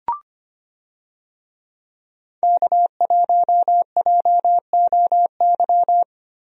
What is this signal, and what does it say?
Morse code, heard as a keyed tone, about 25 wpm, 700 Hz. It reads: K1JOY
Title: USA Call Signs - 5 Characters - Repeated 2x - 25wpm